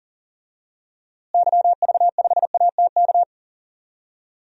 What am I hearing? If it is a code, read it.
Morse code, heard as a keyed tone, about 40 wpm, 700 Hz. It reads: YV5ATK
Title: DXCC-100 Call Signs - 6 Characters - Repeated 3x - 40wpm